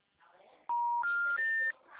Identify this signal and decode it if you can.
Special information tone.
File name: out- -2031- - - .2